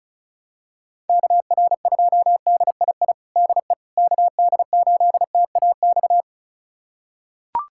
Morse code, heard as a keyed tone, about 36 wpm, 700 Hz. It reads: KR2DII DE KD8TAX
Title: USA Callsign DE Callsign 35wpm